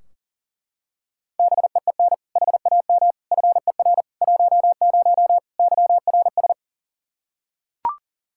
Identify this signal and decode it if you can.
Morse code, heard as a keyed tone, about 40 wpm, 700 Hz. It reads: BEEN HAM FER 10 YRS
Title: Practice QSO - Exchange 3 - More Info - 40wpm